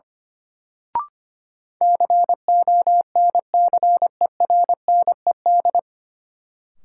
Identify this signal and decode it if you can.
Morse code, heard as a keyed tone, about 25 wpm, 700 Hz. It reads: CONCERNED
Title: Sentences from Top 1000 Words - Review Part 6 - New Word Drill - 25wpm